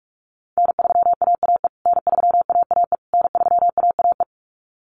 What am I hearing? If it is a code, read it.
Morse code, heard as a keyed tone, about 44 wpm, 700 Hz. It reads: N3AAE N3AAE N3AAE
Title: USA Call Signs Repeated 3x 45wpm